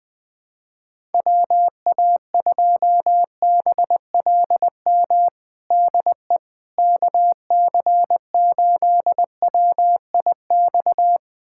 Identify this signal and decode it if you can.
Morse code, heard as a keyed tone, about 20 wpm, 700 Hz. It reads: WA2BLM DE KC8WIX
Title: USA Callsign DE Callsign 20wpm